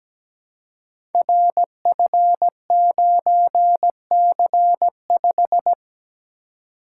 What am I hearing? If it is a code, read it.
Morse code, heard as a keyed tone, about 17 wpm, 700 Hz. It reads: RF9C5
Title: Code Groups - 5 Letter:Numbers - 17wpm